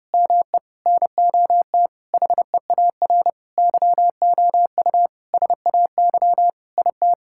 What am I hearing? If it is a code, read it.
Morse code, heard as a keyed tone, about 28 wpm, 700 Hz. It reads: ME NOT HEAR YOU SAY IT
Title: Sentences from Top 1600 Words 30wpm